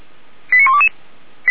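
Five-tone selective call (ZVEI), heard as quick runs of five tones, tones of about 70 ms. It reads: 89122
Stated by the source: ZVEI five-tone selective call received via SDR.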